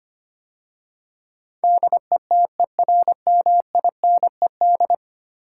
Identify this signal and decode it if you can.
Morse code, heard as a keyed tone, about 25 wpm, 700 Hz. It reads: DETERMINED
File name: 10 Letter Words - First 3 Letters Given - 25wpm